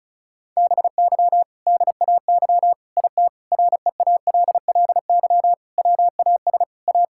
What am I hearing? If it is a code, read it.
Morse code, heard as a keyed tone, about 34 wpm, 700 Hz. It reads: BY DAY IT REALLY WAS A
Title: Sentences from Top 1600 Words 35wpm